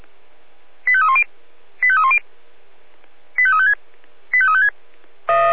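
Five-tone selective call (ZVEI), heard as quick runs of five tones, tones of about 70 ms. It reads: 85310 85310 85356 85356
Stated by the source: ZVEI five-tone call received by SDR.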